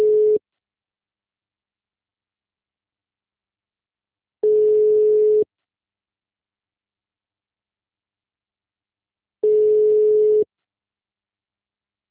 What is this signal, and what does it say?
Ringback tone (Europe, 425 Hz).
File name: out- -2001- - - .149